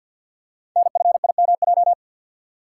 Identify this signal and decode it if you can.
Morse code, heard as a keyed tone, about 49 wpm, 700 Hz. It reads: NFIMJ
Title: Code Groups - 5 Letters - 50wpm